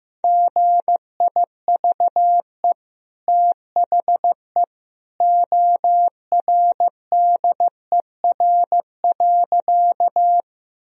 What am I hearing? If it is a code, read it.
Morse code, heard as a keyed tone, about 15 wpm, 700 Hz. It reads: GIVE THE ORDER.
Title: Sentences from Top 1000 Words - Review Part 2 - 15wpm